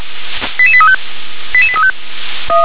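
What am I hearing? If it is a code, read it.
Five-tone selective call (ZVEI), heard as quick runs of five tones, tones of about 70 ms. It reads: 88835 88835
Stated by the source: ZVEI five-tone selective call 88100, received by SDR.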